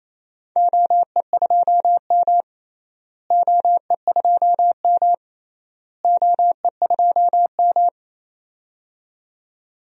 Morse code, heard as a keyed tone, about 28 wpm, 700 Hz. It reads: OE2M OE2M OE2M
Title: DXCC-100 Call Signs - 4 Characters - Repeated 3x - 28wpm